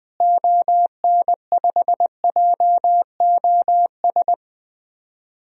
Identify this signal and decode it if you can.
Morse code, heard as a keyed tone, about 20 wpm, 700 Hz. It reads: ON5JOS
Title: DXCC-100 Call Signs - 6 Characters - Repeated 2x - 20wpm